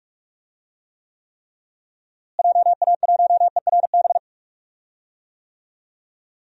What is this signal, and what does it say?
Morse code, heard as a keyed tone, about 46 wpm, 700 Hz. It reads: JA1ERB
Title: DXCC-100 Call Signs - 6 Characters - Repeated 3x - 45wpm